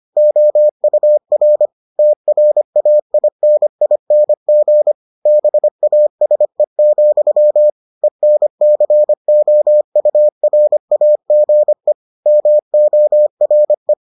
Morse code, heard as a keyed tone, about 25 wpm, 600 Hz. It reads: OUR TRAINING BASE, ENCOURAGE MORE